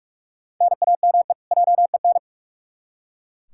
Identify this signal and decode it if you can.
Morse code, heard as a keyed tone, about 45 wpm, 700 Hz. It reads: NAME JEN
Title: K1USN SST QSO Practice 45wpm